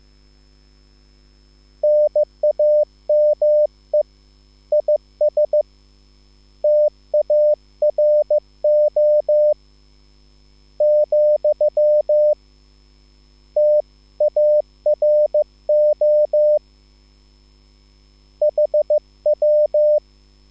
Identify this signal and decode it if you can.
Morse code, heard as a keyed tone, about 15 wpm, spaced as for 13 wpm, 600 Hz. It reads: NAME IS TARO, TARO HW
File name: QSL / CW lesson